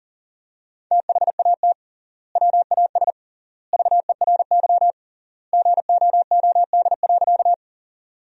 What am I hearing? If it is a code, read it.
Morse code, heard as a keyed tone, about 40 wpm, 700 Hz. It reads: THAT WAS VERY GOOD.
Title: 3x Word-Spacing - Sentences from Top 200 Words - 40wpm